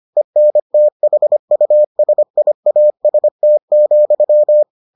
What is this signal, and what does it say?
Morse code, heard as a keyed tone, about 25 wpm, 600 Hz. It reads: ENTHUSIAST,